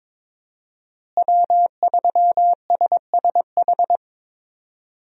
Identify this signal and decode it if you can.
Morse code, heard as a keyed tone, about 22 wpm, 700 Hz. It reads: W3SSH
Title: USA Call Signs - 5 Characters - Repeated 2x - 22wpm